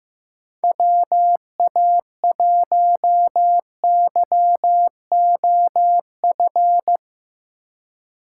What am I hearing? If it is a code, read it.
Morse code, heard as a keyed tone, about 15 wpm, 700 Hz. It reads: WA1YOF
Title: USA Call Signs - 6 Characters - 2x Character Spacing - 15wpm